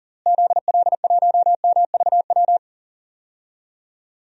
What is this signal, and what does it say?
Morse code, heard as a keyed tone, about 40 wpm, 700 Hz. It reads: ZL1MVW
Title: DXCC-100 Call Signs - 6 Characters - Repeated 3x - 40wpm